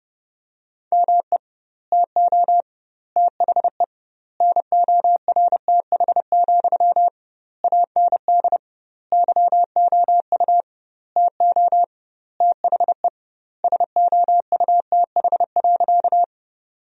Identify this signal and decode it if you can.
Morse code, heard as a keyed tone, about 32 wpm, 700 Hz. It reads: ME TO THE NORTH, AND YOU TO THE SOUTH.
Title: 2x Word-Spacing - Sentences from Top 400 Words - 30wpm